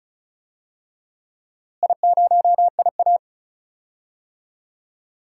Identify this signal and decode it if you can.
Morse code, heard as a keyed tone, about 34 wpm, 700 Hz. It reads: I0IA
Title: DXCC-100 Call Signs - 4 Characters - Repeated 2x - 35wpm